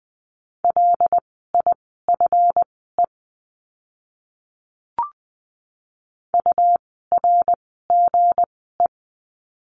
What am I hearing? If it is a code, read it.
Morse code, heard as a keyed tone, about 20 wpm, spaced as for 15 wpm, 700 Hz. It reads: LIFE URGE
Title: 160 Lesson 26f - Letter g - Words 20wpm